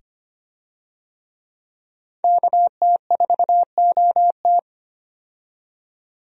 Morse code, heard as a keyed tone, about 25 wpm, 700 Hz. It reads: KT4OT
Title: USA Call Signs - 5 Characters - 25wpm